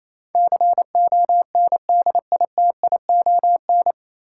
Morse code, heard as a keyed tone, about 28 wpm, 700 Hz. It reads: CONDITION
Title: Sentences from Top 1000 Words - New Word Drill - 28wpm